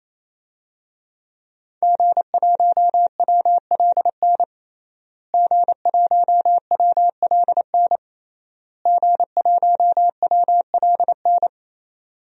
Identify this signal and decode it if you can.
Morse code, heard as a keyed tone, about 28 wpm, 700 Hz. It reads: G1WLN G1WLN G1WLN
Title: DXCC-100 Call Signs - 5 Characters - Repeated 3x - 28wpm